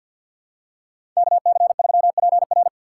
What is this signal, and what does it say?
Morse code, heard as a keyed tone, about 52 wpm, 700 Hz. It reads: KC3PR
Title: USA Call Signs - 5 Characters - 6x Character Spacing - 50wpm